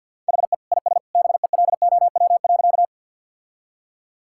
Morse code, heard as a keyed tone, about 50 wpm, 700 Hz. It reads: HE IS BELOW.
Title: Sentences from Top 1000 Words - Review Part 5 - 50wpm